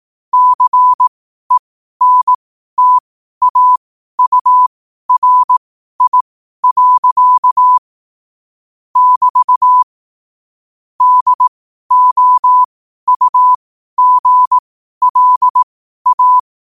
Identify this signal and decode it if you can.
Morse code, heard as a keyed tone, about 18 wpm, spaced as for 12 wpm, 1000 Hz. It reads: CENTAURI. <BT> DOUGLA
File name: qotdfiles / 12wpm